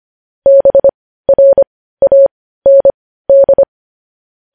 Morse code, heard as a keyed tone, about 25 wpm, spaced as for 15 wpm, 550 Hz. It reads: BRAND
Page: News Headlines in Morse Code at 15 WPM